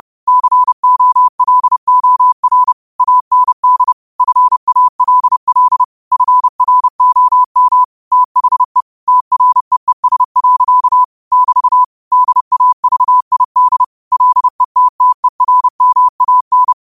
Morse code, heard as a keyed tone, about 30 wpm, 1000 Hz. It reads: COLOR AND FALL FROM THE TREES. <BT> DAVID LETTERMAN
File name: 30wpm